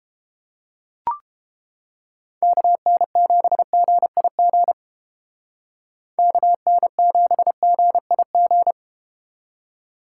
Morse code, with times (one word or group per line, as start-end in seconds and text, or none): 2.4-4.7 KN7GIG
6.2-8.7 KN7GIG